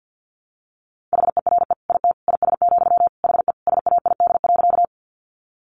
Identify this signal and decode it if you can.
Morse code, heard as a keyed tone, about 50 wpm, 700 Hz. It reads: HERE IT IS, HE SAID.
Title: Sentences from Top 200 Words 50wpm